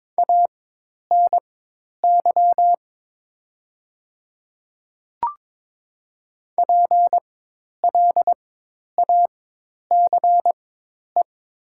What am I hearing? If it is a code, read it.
Morse code, heard as a keyed tone, about 22 wpm, spaced as for 10 wpm, 700 Hz. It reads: ANY PLACE
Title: 4x Character Spacing - Top 200 Words - 22wpm